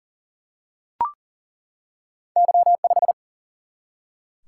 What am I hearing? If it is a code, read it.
Morse code, heard as a keyed tone, about 40 wpm, 700 Hz. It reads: Y5
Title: Single-Letter Single-Number 40wpm